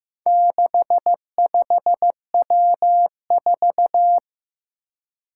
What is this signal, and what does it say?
Morse code, heard as a keyed tone, about 15 wpm, 700 Hz. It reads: 65W4
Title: Four Challenging Letter-Number Combinations 15wpm